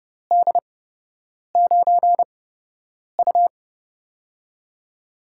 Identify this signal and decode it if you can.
Morse code, heard as a keyed tone, about 30 wpm, spaced as for 8 wpm, 700 Hz. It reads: D9U
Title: USA Call Signs - 4 Characters - 8x Character Spacing - 30wpm